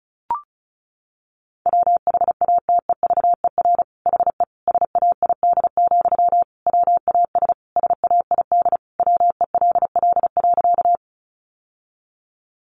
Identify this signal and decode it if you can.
Morse code, heard as a keyed tone, about 34 wpm, 700 Hz. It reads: WHATEVER HE SAID, WAS SAID WELL.
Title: Sentences from Top 1100 Words 35wpm